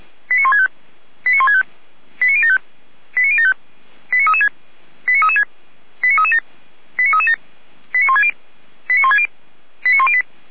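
Five-tone selective call (ZVEI), heard as quick runs of five tones, tones of about 80 ms. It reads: 89165 89165 89075 89075 89336 89336 89337 89337 89160 89160 89108
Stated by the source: ZVEI five-tone call received by SDR.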